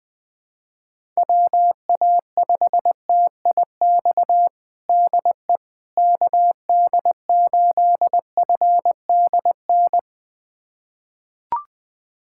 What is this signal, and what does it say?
Morse code, heard as a keyed tone, about 20 wpm, 700 Hz. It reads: WA5TIX DE KD8FDN
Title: USA Callsign DE Callsign - 6 Characters - 20wpm